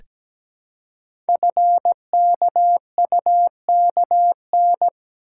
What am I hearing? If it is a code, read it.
Morse code, heard as a keyed tone, about 17 wpm, 700 Hz. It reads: FKUKN
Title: Code Groups - 5 Letters - 17wpm